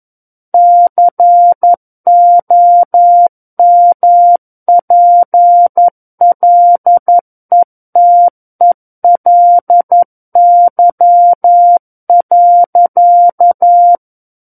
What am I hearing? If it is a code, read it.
Morse code, heard as a keyed tone, about 11 wpm, 700 Hz. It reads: COMPLETELY.
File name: Book One / 02-Chapter 2 - 11 WPM